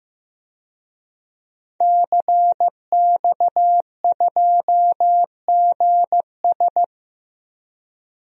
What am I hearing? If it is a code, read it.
Morse code, heard as a keyed tone, about 15 wpm, 700 Hz. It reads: CX2GS
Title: DXCC-100 Call Signs - 5 Characters - 15wpm